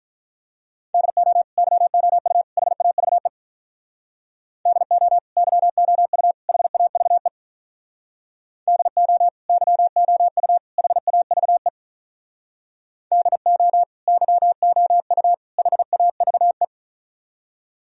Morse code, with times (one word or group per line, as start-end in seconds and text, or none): 0.9-1.4 DO
1.6-2.4 YOU
2.6-3.3 HAVE
4.7-5.2 DO
5.4-6.3 YOU
6.5-7.3 HAVE
8.7-9.3 DO
9.5-10.6 YOU
10.8-11.7 HAVE
13.1-13.8 DO
14.1-15.3 YOU
15.6-16.6 HAVE